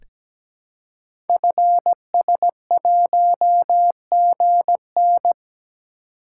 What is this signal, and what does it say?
Morse code, heard as a keyed tone, about 17 wpm, 700 Hz. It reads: FS1GN
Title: Code Groups - 5 Letter:Numbers - 17wpm